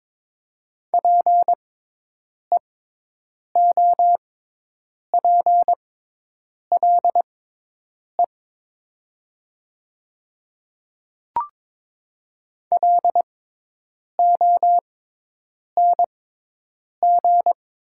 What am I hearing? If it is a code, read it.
Morse code, heard as a keyed tone, about 22 wpm, spaced as for 8 wpm, 700 Hz. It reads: PEOPLE LONG
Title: 6x Character Spacing - Top 300 Words - 22wpm